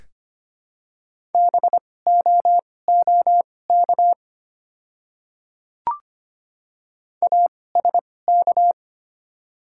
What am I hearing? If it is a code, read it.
Morse code, heard as a keyed tone, about 25 wpm, spaced as for 18 wpm, 690 Hz. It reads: BOOK ASK
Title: 2x Character Spacing - Top 300 Words - 25wpm